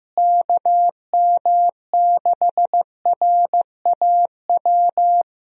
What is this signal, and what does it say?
Morse code, heard as a keyed tone, about 15 wpm, 700 Hz. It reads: KM6RAW
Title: USA Call Signs - 6 Characters - Repeated 3x - 15wpm